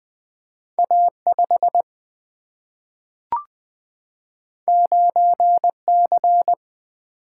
Morse code, heard as a keyed tone, about 20 wpm, 700 Hz. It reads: A5 9C
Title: Single-Letter Single-Number 20wpm